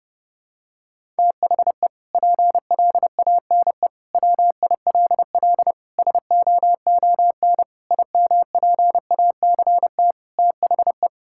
Morse code, heard as a keyed tone, about 30 wpm, 700 Hz. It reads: THE PLANE WILL SOON IMPACT THE
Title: Sentences from Top 1000 Words - Review Part 6 - 30wpm